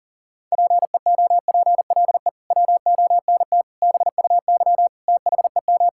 Morse code, heard as a keyed tone, about 40 wpm, 700 Hz. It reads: PEOPLE WONT BUY THEM
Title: Sentences from Top 1000 Words - Review Part 6 - 40wpm